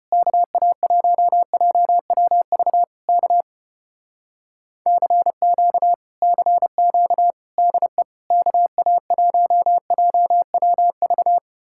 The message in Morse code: KA1JWV K CQ CQ DE KA1JWV

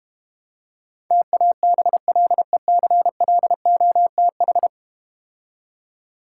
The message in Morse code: TABLECLOTH

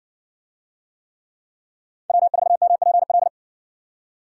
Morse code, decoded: W4MPL

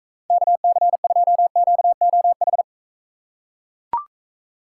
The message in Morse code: KC2QOH

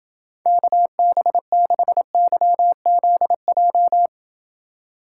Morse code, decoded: KB6YZJ

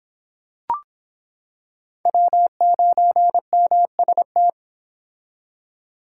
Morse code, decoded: W9MST

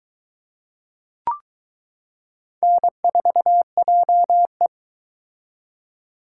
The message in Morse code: N4JE

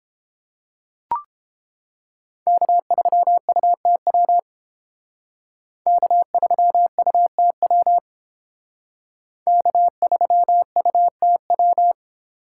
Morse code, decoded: K3UTW K3UTW K3UTW

